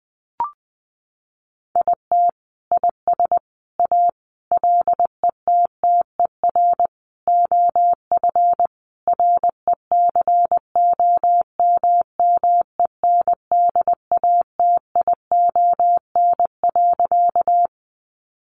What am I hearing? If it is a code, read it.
Morse code, heard as a keyed tone, about 20 wpm, 700 Hz. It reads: IT IS A LETTER OF RECOMMENDATION.
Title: Sentences from Top 1400 Words 20wpm